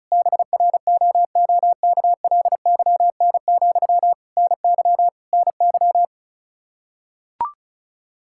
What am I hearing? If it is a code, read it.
Morse code, heard as a keyed tone, about 36 wpm, 700 Hz. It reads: BROOKLYN, NY NY
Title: Practice QSO - Exchange 1 - Beginning - 35wpm